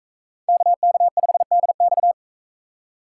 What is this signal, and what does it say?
Morse code, heard as a keyed tone, about 44 wpm, 700 Hz. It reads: KK5DX